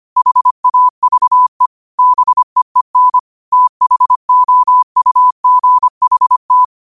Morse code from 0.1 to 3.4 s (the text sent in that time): HAVE BEEN